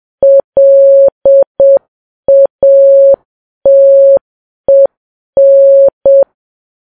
Morse code, heard as a keyed tone, about 7 wpm, 550 Hz. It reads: LATEN
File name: Files Web tls / Morse on9fv / morse7kps